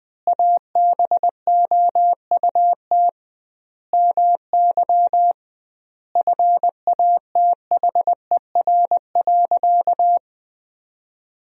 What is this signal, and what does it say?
Morse code, heard as a keyed tone, about 21 wpm, 700 Hz. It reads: ABOUT MY FATHER.
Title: 2x Word-Spacing - Sentences from Top 400 Words - 20wpm